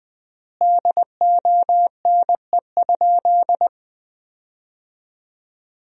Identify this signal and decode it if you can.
Morse code, heard as a keyed tone, about 20 wpm, 700 Hz. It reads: DONE?
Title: 3x Word-Spacing - Sentences from Top 500 Words - 20wpm